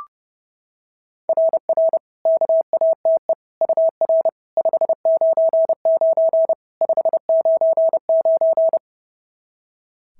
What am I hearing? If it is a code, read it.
Morse code, heard as a keyed tone, about 30 wpm, 660 Hz. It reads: RR KATE UR 599 599
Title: SOTA QSO Practice 30wpm